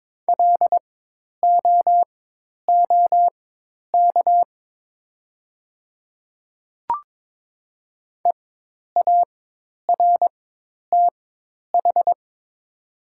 Morse code, heard as a keyed tone, about 22 wpm, spaced as for 10 wpm, 700 Hz. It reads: LOOK EARTH
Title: 4x Character Spacing - Top 500 Words - 22wpm